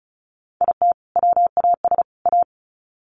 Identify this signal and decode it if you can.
Morse code, heard as a keyed tone, about 34 wpm, 700 Hz. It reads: IT WAS A
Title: Sets of 3 Words 35wpm